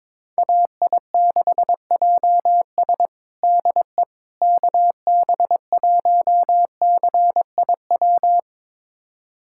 Morse code, heard as a keyed tone, about 22 wpm, 700 Hz. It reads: AI6JS DE KB1CIW
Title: USA Callsign DE Callsign 22wpm